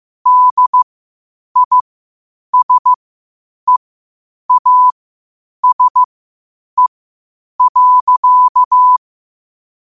Morse code, heard as a keyed tone, about 15 wpm, spaced as for 8 wpm, 1000 Hz. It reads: DISEASE.
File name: qotdfiles / 8wpm